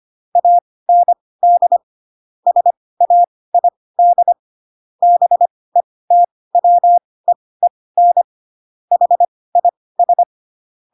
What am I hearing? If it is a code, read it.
Morse code, heard as a keyed tone, about 25 wpm, spaced as for 18 wpm, 700 Hz. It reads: AND SAID BETWEEN HIS
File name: Pinocchio ch3 12 25 ik1hge